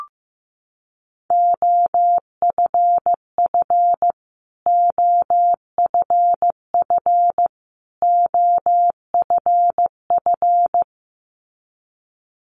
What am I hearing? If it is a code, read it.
Morse code, heard as a keyed tone, about 15 wpm, 700 Hz. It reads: OFF OFF OFF
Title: Top 500 Words Repeated 15wpm